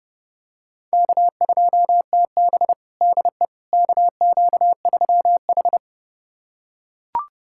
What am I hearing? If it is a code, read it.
Morse code, heard as a keyed tone, about 32 wpm, 700 Hz. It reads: K2TB DE KQ3H
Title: USA Callsign DE Callsign - 4 Characters - 30wpm